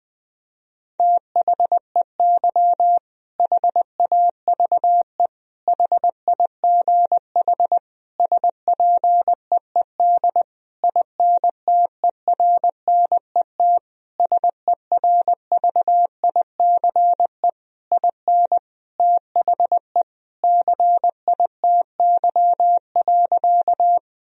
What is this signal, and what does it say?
Morse code, heard as a keyed tone, about 20 wpm, 700 Hz. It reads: THEY HAVE HIGH SPEED INTERNET SERVICE IN THE CITY.
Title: Sentences from Top 1000 Words - Review Part 4 - 20wpm